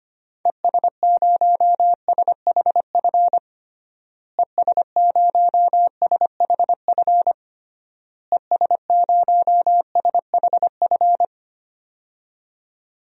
Morse code, heard as a keyed tone, about 25 wpm, 700 Hz. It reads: ES0SHF ES0SHF ES0SHF
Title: DXCC-100 Call Signs - 6 Characters - Repeated 3x - 25wpm